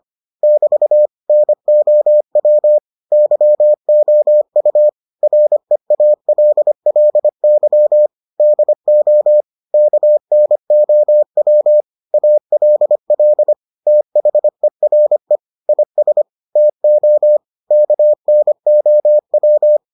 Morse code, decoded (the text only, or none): <BT> NOW YOU REALLY DO KNOW ALL THERE IS TO KNOW